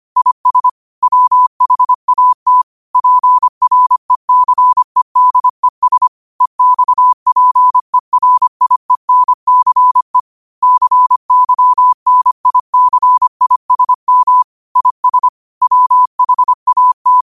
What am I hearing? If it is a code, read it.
Morse code, heard as a keyed tone, about 25 wpm, 1000 Hz. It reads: IS WHAT PRECEDES EXPERIENCE CYNICISM IS WHAT
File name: 25wpm